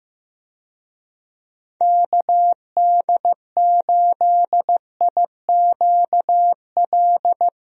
Morse code, heard as a keyed tone, about 16 wpm, 700 Hz. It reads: KD8IQL
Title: USA Call Signs - 6 Characters - Repeated 3x - 15wpm